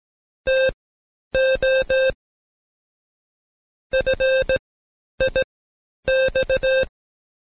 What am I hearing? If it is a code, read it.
Morse code, heard as a keyed tone, about 17 wpm, spaced as for 10 wpm, 530 Hz. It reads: TO FIX
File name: Code Quick / AudioDownld2 / Seg131-Stepper4 9WPM